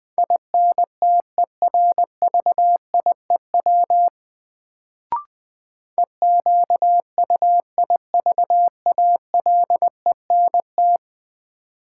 Morse code, heard as a keyed tone, about 20 wpm, 700 Hz. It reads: INTERVIEW EQUIVALENT
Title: Sentences from Top 1600 Words - New Word Drill Alt - 20wpm